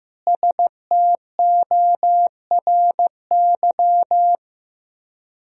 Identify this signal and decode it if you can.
Morse code, heard as a keyed tone, about 16 wpm, 700 Hz. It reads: STORY
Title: 4x Word-Spacing - Sentences from Top 300 Words - 15wpm